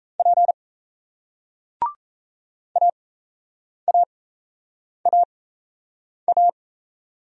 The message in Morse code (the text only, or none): P AAAA